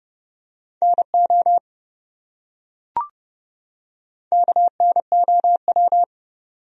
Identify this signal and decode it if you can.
Morse code, heard as a keyed tone, about 30 wpm, 700 Hz. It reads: NO KNOW
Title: Top 100 Words 30wpm